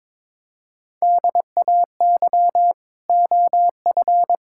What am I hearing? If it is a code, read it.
Morse code, heard as a keyed tone, about 22 wpm, 700 Hz. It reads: DAY OF
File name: practice / 22wpm / Sets of 8 Words - 22wpm